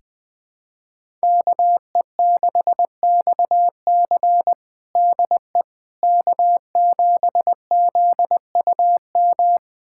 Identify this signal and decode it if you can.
Morse code, heard as a keyed tone, about 21 wpm, 700 Hz. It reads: KE6XC DE K7ZUM
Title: USA Callsign DE Callsign - 5 Characters - 20wpm